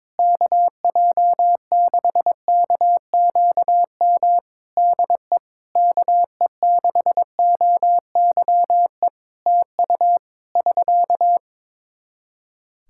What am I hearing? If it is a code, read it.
Morse code, heard as a keyed tone, about 22 wpm, 700 Hz. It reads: KJ6KQM DE KE6OYE TU <SK>
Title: Practice QSO - Exchange 4 - Ending - 22wpm